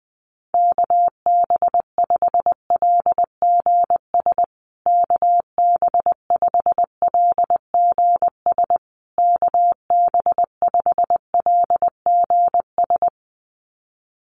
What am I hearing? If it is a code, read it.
Morse code, heard as a keyed tone, about 20 wpm, 700 Hz. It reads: KB5LGS KB5LGS KB5LGS
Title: USA Call Signs Repeated 3x 20wpm